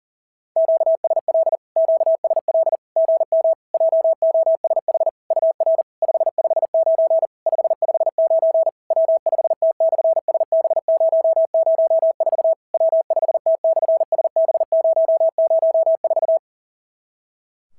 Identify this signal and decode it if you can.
Morse code, heard as a keyed tone, about 40 wpm, 650 Hz. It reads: QSL QSL GM JOSH UR 559 559 W5T/SB004 W5T/SB004
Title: SOTA QSO Practice 40wpm